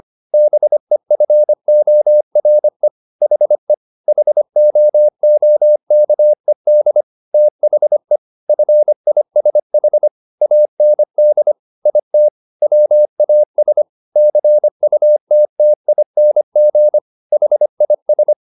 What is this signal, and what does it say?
Morse code, heard as a keyed tone, about 25 wpm, 600 Hz. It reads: BEFORE HE HOOKED THE FISH AND IT WAS CUTTING HIS
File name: the old man and the sea part 1 25wpm